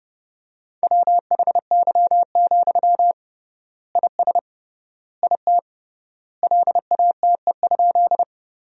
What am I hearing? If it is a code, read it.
Morse code, heard as a keyed tone, about 30 wpm, 700 Hz. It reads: WHY, IS IT LATE?
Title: 3x Word-Spacing - Sentences from Top 300 Words - 30wpm